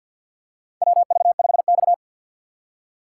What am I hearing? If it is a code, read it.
Morse code, heard as a keyed tone, about 53 wpm, 700 Hz. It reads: WV5X
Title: USA Call Signs - 6x Character Spacing - 50wpm